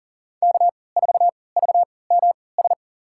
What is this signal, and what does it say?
Morse code, heard as a keyed tone, about 40 wpm, spaced as for 23 wpm, 700 Hz. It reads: K4VMS
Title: USA Call Signs - 5 Characters - 3x Character Spacing - 40wpm